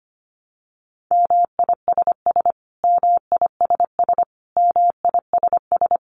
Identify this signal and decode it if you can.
Morse code, heard as a keyed tone, about 25 wpm, 700 Hz. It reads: MISS MISS MISS
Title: Top 500 Words Repeated 25wpm